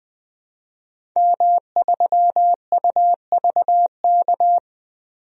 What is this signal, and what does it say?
Morse code, heard as a keyed tone, about 21 wpm, 700 Hz. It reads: M3UVK
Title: DXCC-100 Call Signs - 5 Characters - 20wpm